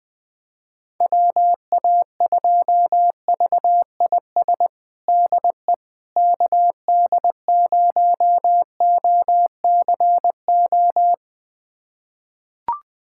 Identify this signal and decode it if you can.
Morse code, heard as a keyed tone, about 20 wpm, 700 Hz. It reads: WA2VIS DE KD0OCO
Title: USA Callsign DE Callsign - 6 Characters - 20wpm